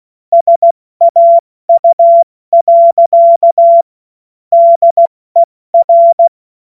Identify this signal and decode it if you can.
Morse code, heard as a keyed tone, about 16 wpm, spaced as for 14 wpm, 680 Hz. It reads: SAU. DER
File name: Morsekurs / Witze / Witz 03 Juni Mai 2025 70 F80